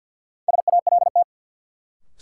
Morse code, heard as a keyed tone, about 46 wpm, 700 Hz. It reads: SALT
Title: Sentences from Top 800 Words - New Word Drill - 50wpm